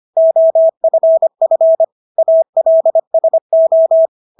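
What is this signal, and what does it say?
Morse code, heard as a keyed tone, about 25 wpm, 650 Hz. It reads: OFF ALSO
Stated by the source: Morse practice of English words at 25 wpm from CW med Gnister.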